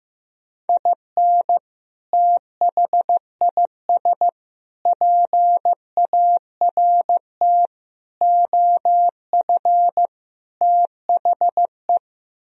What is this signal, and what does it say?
Morse code, heard as a keyed tone, about 15 wpm, 700 Hz. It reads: IN THIS PART OF THE
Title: Speed-Racing - Sets of 5 Words 15wpm